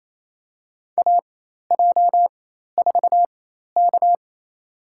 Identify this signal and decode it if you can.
Morse code, heard as a keyed tone, about 28 wpm, spaced as for 13 wpm, 700 Hz. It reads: AJ4K
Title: USA Call Signs - 4 Characters - 4x Character Spacing - 28wpm